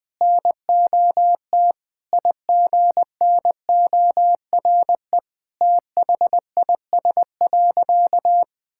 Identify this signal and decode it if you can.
Morse code, heard as a keyed tone, about 20 wpm, 700 Hz. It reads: NOT IGNORE THIS.
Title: Sentences with 3-Word Phrases - 20wpm